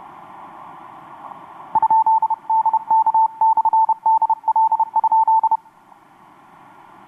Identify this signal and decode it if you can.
Morse code, heard as a keyed tone, about 31 wpm, 890 Hz. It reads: ?DK/DL?
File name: xw2cw rx6am etc 12m cw 26nov2012 1153